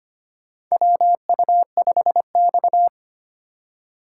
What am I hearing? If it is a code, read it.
Morse code, heard as a keyed tone, about 25 wpm, 700 Hz. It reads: WU5X
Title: USA Call Signs - 2x Character Spacing - 25wpm